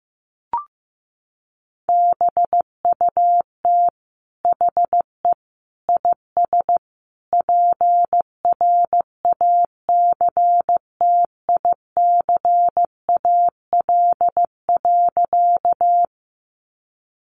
Morse code, heard as a keyed tone, about 15 wpm, 700 Hz. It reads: BUT HE IS PRACTICAL.
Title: Sentences from Top 1400 Words 15wpm